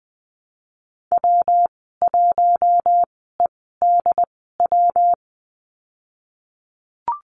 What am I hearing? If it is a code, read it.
Morse code, heard as a keyed tone, about 20 wpm, spaced as for 15 wpm, 700 Hz. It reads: W1EDW
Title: 154 Lesson 25g - Letter p - Call Signs 20wpm